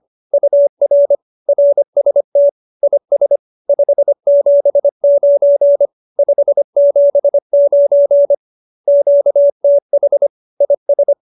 Morse code, heard as a keyed tone, about 25 wpm, 570 Hz. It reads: UR RST IS 579 579 QTH IS